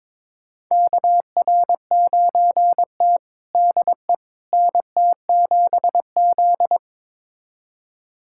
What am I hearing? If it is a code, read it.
Morse code, heard as a keyed tone, about 21 wpm, 700 Hz. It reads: KR9T DE NT7Z
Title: USA Callsign DE Callsign - 4 Characters - 22wpm